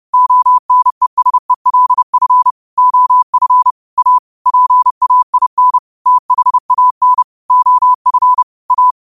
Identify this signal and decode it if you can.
Morse code, heard as a keyed tone, about 31 wpm, 1000 Hz. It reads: ONESELF OF A PAIN THAN OF A
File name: qotdfiles / 30wpm